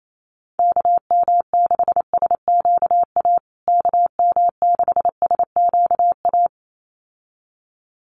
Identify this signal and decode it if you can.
Morse code, heard as a keyed tone, about 28 wpm, 700 Hz. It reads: KM6SQA KM6SQA
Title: USA Call Signs Repeated 2x 28wpm